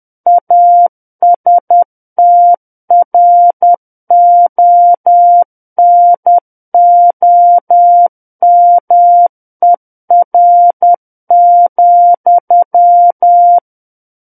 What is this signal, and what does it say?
Morse code, heard as a keyed tone, about 10 wpm, 700 Hz. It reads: ASTRONOMER,
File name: Book One / 01-Chapter 1 - 10 WPM